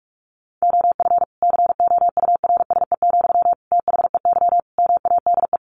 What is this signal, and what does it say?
Morse code, heard as a keyed tone, about 44 wpm, 700 Hz. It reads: OF COURSE, THEY MADE
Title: Sentences from Top 1400 Words 45wpm